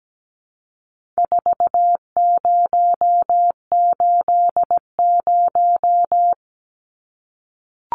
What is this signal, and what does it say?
Morse code, heard as a keyed tone, about 17 wpm, 700 Hz. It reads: 4080
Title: Numbers - 4-Digit 17wpm